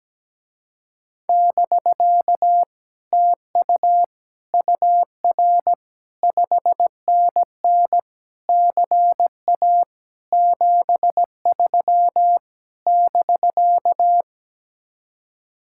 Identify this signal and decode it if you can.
Morse code, heard as a keyed tone, about 17 wpm, 700 Hz. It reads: <BK> TU UR 5NN CA 73 <BK>
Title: Speed-Racing Common QSO Phrases - 17wpm